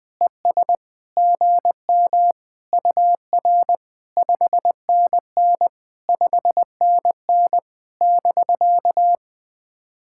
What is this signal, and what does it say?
Morse code, heard as a keyed tone, about 20 wpm, 700 Hz. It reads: ES GM UR 5NN 5NN <BK>
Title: POTA QSO Practice - 20wpm